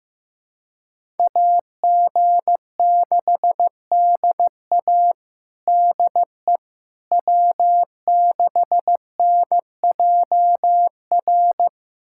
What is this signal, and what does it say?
Morse code, heard as a keyed tone, about 15 wpm, 700 Hz. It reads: AG6DA DE W6NJR
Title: USA Callsign DE Callsign - 5 Characters - 15wpm